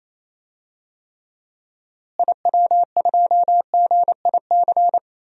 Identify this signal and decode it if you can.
Morse code, heard as a keyed tone, about 28 wpm, 700 Hz. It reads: IW2GIC
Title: DXCC-100 Call Signs - 6 Characters - Repeated 2x - 28wpm